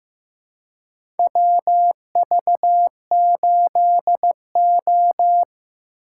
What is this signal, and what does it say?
Morse code, heard as a keyed tone, about 15 wpm, 700 Hz. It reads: WV8O
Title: USA Call Signs - 4 Characters - 3x Character Spacing - 15wpm